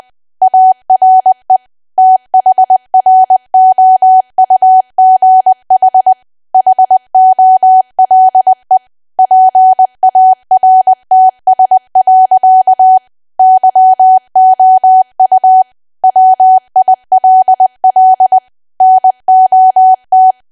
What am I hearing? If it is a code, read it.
Morse code, heard as a keyed tone, about 20 wpm, 750 Hz. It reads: ARE THROUGH HOLE PARTS. YOU WILL NOT
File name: cwa / W1AW-Level3 / 20WPM